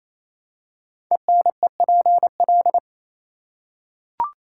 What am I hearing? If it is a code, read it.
Morse code, heard as a keyed tone, about 27 wpm, 700 Hz. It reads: ENEPL
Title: Code Groups - 5 Letters - 28wpm